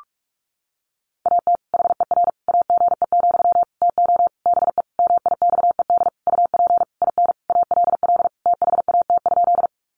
Morse code, heard as a keyed tone, about 44 wpm, 700 Hz. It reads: AT HER AGE, TO BE MIXED UP IN ALL THAT?
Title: Sentences from Top 1500 Words 45wpm